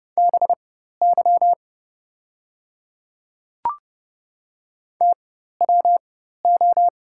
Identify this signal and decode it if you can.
Morse code, heard as a keyed tone, about 30 wpm, spaced as for 14 wpm, 690 Hz. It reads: BY TWO
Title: 4x Character Spacing - Top 100 Words - 30wpm